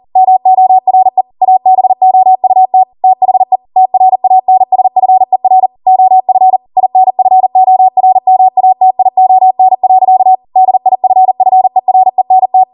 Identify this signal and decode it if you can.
Morse code, heard as a keyed tone, about 40 wpm, 750 Hz. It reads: MORE ABOUT THE TRANSFER OF INFORMATION. DIFFERENT